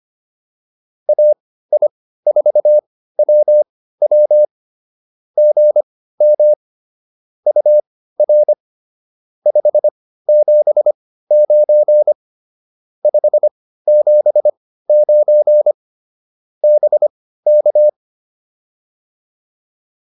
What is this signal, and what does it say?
Morse code, heard as a keyed tone, about 25 wpm, spaced as for 15 wpm, 600 Hz. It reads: AI4WW GM UR 579 579 BK